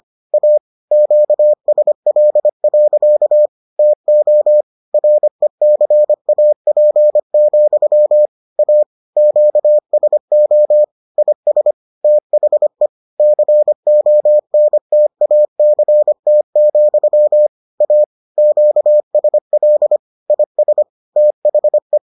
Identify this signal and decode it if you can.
Morse code, heard as a keyed tone, about 25 wpm, 600 Hz. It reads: A QSL. TO RECAP, A QSO IS THE CONTACT, A QSL IS THE